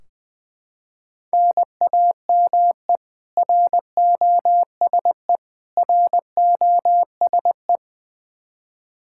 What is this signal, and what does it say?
Morse code, heard as a keyed tone, about 21 wpm, 700 Hz. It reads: NAME ROSE ROSE
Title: Practice QSO - Exchange 1 - Beginning - 20wpm